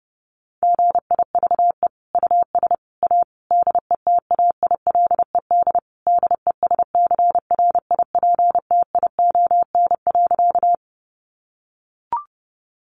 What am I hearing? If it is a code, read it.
Morse code, heard as a keyed tone, about 30 wpm, 700 Hz. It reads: GIVE US A DETAILED DESCRIPTION.
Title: Sentences from Top 1500 Words 30wpm